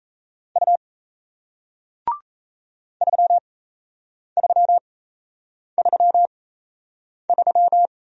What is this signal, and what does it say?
Morse code, heard as a keyed tone, about 38 wpm, spaced as for 8 wpm, 700 Hz. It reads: U 3333